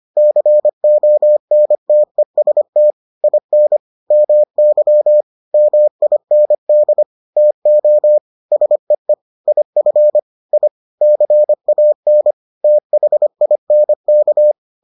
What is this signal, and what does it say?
Morse code, heard as a keyed tone, about 25 wpm, 600 Hz. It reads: CONTEST IN MY MIND TO SEE IF I CAN THINK